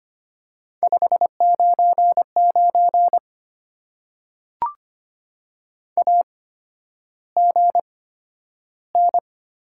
Morse code, heard as a keyed tone, about 25 wpm, spaced as for 11 wpm, 690 Hz. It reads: 599 AGN
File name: Top 90 QSO Elements - 8x Character Spacing - 25wpm